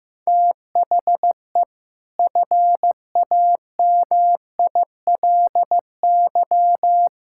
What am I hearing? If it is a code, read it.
Morse code, heard as a keyed tone, about 16 wpm, 700 Hz. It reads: THE FAMILY
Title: Sentences from Top 1000 Words - Review Part 5 - 15wpm